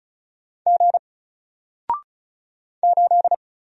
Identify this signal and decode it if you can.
Morse code, heard as a keyed tone, about 35 wpm, 700 Hz. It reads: G 8
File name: Single-Letter-Number-35wpm